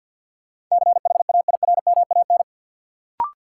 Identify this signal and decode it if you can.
Morse code, heard as a keyed tone, about 53 wpm, 700 Hz. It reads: CHAIRMAN
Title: Sets of 3 Words - New Word Drill - 50wpm